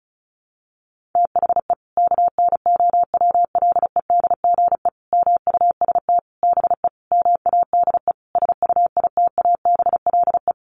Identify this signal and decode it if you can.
Morse code, heard as a keyed tone, about 34 wpm, 700 Hz. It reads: THE KNOWLEDGE MUST BE MADE SUITABLE
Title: Sentences from Top 1400 Words 35wpm